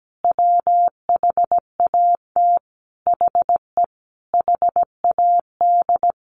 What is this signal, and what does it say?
Morse code, heard as a keyed tone, about 17 wpm, 700 Hz. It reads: WHAT HE HAD
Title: Sentences from Top 1100 Words 17wpm